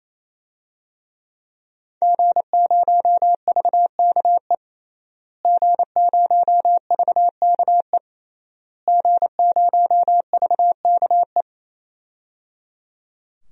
Morse code, heard as a keyed tone, about 28 wpm, 690 Hz. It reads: G0VKE G0VKE G0VKE
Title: DXCC-100 Call Signs - 5 Characters - Repeated 3x - 28wpm